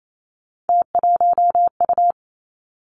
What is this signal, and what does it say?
Morse code, heard as a keyed tone, about 28 wpm, 700 Hz. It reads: T1U
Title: Three Challenging Letter-Number Combinations 28wpm